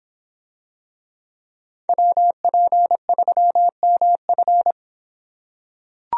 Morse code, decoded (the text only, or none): WP3MF